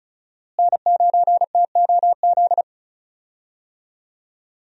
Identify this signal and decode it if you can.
Morse code, heard as a keyed tone, about 36 wpm, 700 Hz. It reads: N9TOZ
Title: USA Call Signs - 5 Characters - Repeated 3x - 35wpm